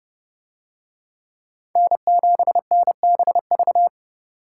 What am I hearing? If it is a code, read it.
Morse code, heard as a keyed tone, about 30 wpm, 700 Hz. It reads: N7NBV